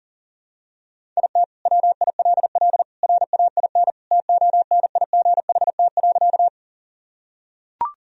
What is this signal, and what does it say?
Morse code, heard as a keyed tone, about 37 wpm, 700 Hz. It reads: IT WILL RAIN TONIGHT.
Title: Sentences from Top 1000 Words - Review Part 1 - 40wpm